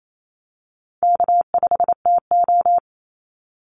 Morse code, heard as a keyed tone, about 28 wpm, 700 Hz. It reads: K5TO
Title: Easy Call Signs Repeated 2x - 28wpm